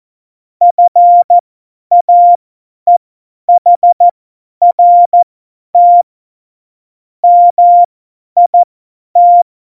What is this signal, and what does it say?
Morse code, heard as a keyed tone, about 14 wpm, spaced as for 10 wpm, 700 Hz. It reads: FAEHRT MIT
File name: Morsekurs / Witze / Witz 14 Juli 2020, 50 F70